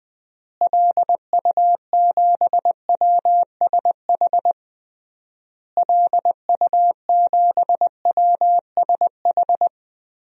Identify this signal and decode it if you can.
Morse code, heard as a keyed tone, about 20 wpm, 700 Hz. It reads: LU7WSH LU7WSH
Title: DXCC-100 Call Signs - 6 Characters - Repeated 3x - 20wpm